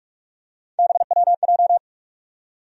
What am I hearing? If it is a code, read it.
Morse code, heard as a keyed tone, about 47 wpm, 700 Hz. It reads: BWJ
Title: USA Call Sign Postfixes - 3 Letters 45wpm